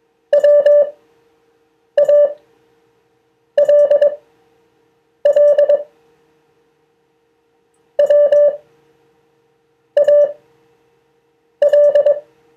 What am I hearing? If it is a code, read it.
Morse code, heard as a keyed tone, about 22 wpm, spaced as for 6 wpm, 580 Hz. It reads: WALL WAL
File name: 2of4-words-1